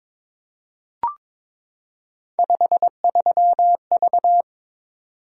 Morse code, heard as a keyed tone, about 23 wpm, 700 Hz. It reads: 53V
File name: Three Letter-Number Combinations - 22wpm